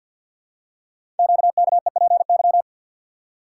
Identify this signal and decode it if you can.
Morse code, heard as a keyed tone, about 51 wpm, 700 Hz. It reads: QKEPY
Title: Code Groups - 5 Letters - 50wpm